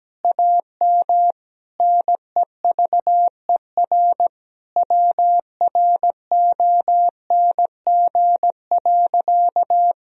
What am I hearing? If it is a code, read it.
Morse code, heard as a keyed tone, about 17 wpm, 700 Hz. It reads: AM NEVER WRONG.
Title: Sentences from Top 1000 Words - Review Part 3 - 17wpm